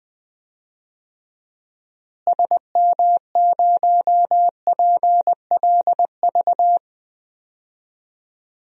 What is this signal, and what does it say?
Morse code, heard as a keyed tone, about 20 wpm, 700 Hz. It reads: SM0PLV
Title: DXCC-100 Call Signs - 6 Characters - Repeated 2x - 20wpm